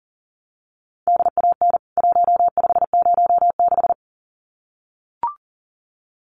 Morse code, heard as a keyed tone, about 40 wpm, 700 Hz. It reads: DAN 1506
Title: CWT Exchange 40wpm